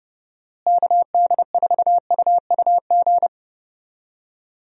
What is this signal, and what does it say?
Morse code, heard as a keyed tone, about 29 wpm, 700 Hz. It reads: KD4UUG
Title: USA Call Signs - 6 Characters - Repeated 2x - 30wpm